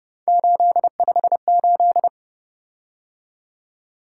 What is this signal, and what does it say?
Morse code, heard as a keyed tone, about 29 wpm, 700 Hz. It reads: 858